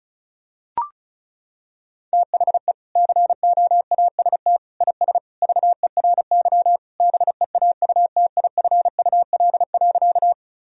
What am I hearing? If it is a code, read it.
Morse code, heard as a keyed tone, about 34 wpm, 700 Hz. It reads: THE COAST IS VERY BEAUTIFUL.
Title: Sentences from Top 1000 Words - Review Part 4 - 35wpm